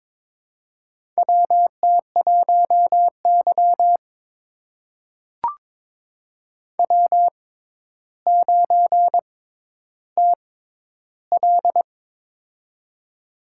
Morse code, heard as a keyed tone, about 22 wpm, spaced as for 11 wpm, 700 Hz. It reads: WT1Y W9TL
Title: USA Call Signs - 4 Characters - 6x Character Spacing - 22wpm